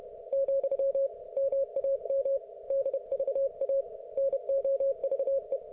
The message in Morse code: , MAM DVA NOVE

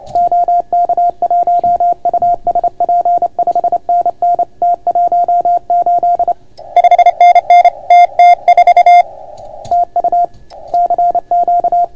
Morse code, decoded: OK1USP 5NN T18 5NN TT4 TU CQ

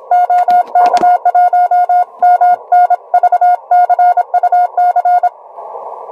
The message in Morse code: OK1MNV CUC

